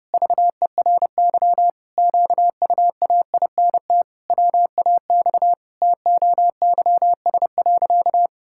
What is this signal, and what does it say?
Morse code, heard as a keyed tone, about 30 wpm, 700 Hz. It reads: VERY QUAINT WAX TOYS.
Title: Pangram Challenge v2 - 30wpm